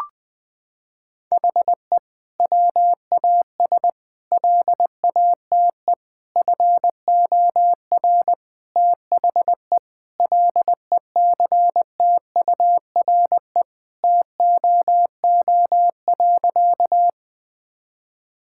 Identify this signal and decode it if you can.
Morse code, heard as a keyed tone, about 21 wpm, 700 Hz. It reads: HE WAS LATE FOR THE LECTURE TOO.
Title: Sentences from Top 1600 Words 20wpm